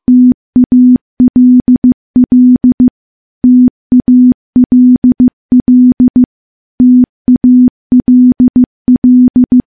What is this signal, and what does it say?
Morse code, heard as a keyed tone, about 15 wpm, 250 Hz. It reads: TALL TALL TALL
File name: morse-training-15WPM